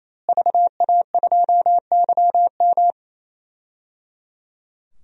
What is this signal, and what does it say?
Morse code, heard as a keyed tone, about 28 wpm, 700 Hz. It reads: VA2YM
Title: DXCC-100 Call Signs - 5 Characters - Repeated 3x - 28wpm